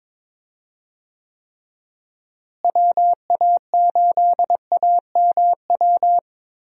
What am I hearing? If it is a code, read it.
Morse code, heard as a keyed tone, about 21 wpm, 700 Hz. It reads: WA8AMW
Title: USA Call Signs - 6 Characters - Repeated 3x - 22wpm